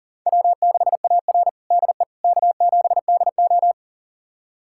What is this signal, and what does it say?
Morse code, heard as a keyed tone, about 40 wpm, 700 Hz. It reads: W6AR DE K7DO
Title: USA Callsign DE Callsign - 4 Characters - 40wpm